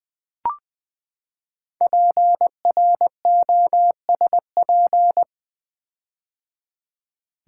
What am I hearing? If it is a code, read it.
Morse code, heard as a keyed tone, about 20 wpm, 700 Hz. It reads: PROSP
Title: Build A Word - 5 Letters At A Time - 10-Letter Words - 20wpm